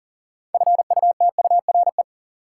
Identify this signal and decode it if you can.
Morse code, heard as a keyed tone, about 40 wpm, 700 Hz. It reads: FUTURE
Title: Sentences from Top 700 Words - New Word Drill - 40wpm